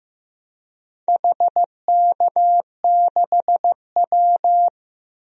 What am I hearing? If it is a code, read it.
Morse code, heard as a keyed tone, about 14 wpm, 700 Hz. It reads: HK6W
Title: DXCC-100 Call Signs - 4 Characters - Repeated 2x - 15wpm